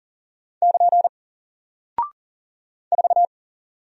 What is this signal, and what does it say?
Morse code, heard as a keyed tone, about 39 wpm, 700 Hz. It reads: <KN> 4
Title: Single Character (Letter:Number:Punctuation:Prosign) 40wpm